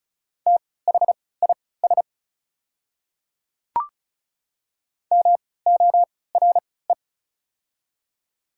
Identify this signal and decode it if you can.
Morse code, heard as a keyed tone, about 36 wpm, spaced as for 20 wpm, 690 Hz. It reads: THIS MORE
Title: 3x Character Spacing - Top 100 Words - 35wpm